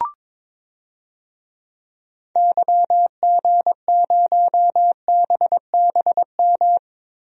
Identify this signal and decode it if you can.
Morse code, heard as a keyed tone, about 22 wpm, 700 Hz. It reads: YG0BBM
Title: DXCC-100 Call Signs - 6 Characters - Repeated 3x - 22wpm